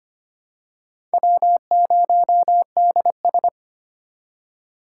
Morse code, heard as a keyed tone, about 25 wpm, 700 Hz. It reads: W0DS
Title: USA Call Signs - 4 Characters - 3x Character Spacing - 25wpm